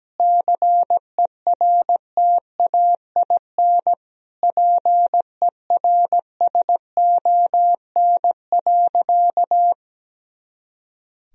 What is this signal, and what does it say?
Morse code, heard as a keyed tone, about 17 wpm, 700 Hz. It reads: CERTAIN PERSON.
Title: Sentences from Top 500 Words and Abbreviations 17wpm